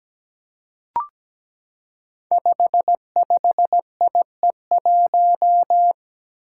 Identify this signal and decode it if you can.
Morse code, heard as a keyed tone, about 17 wpm, 700 Hz. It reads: 55IE1
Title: Five Challenging Letter-Number Combinations 17wpm